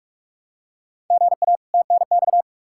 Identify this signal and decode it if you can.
Morse code, heard as a keyed tone, about 42 wpm, 700 Hz. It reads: GA TNX
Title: Practice QSO - Exchange 1 - Beginning - 45wpm